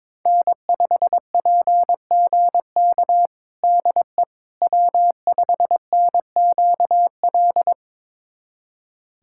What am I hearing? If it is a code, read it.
Morse code, heard as a keyed tone, about 22 wpm, 700 Hz. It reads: N5PGK DE W5NQL
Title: USA Callsign DE Callsign - 5 Characters - 22wpm